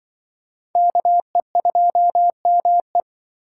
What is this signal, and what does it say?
Morse code, heard as a keyed tone, about 24 wpm, 700 Hz. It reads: KE2ME
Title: Speed Racing - USA 2x2 Call Signs 20wpm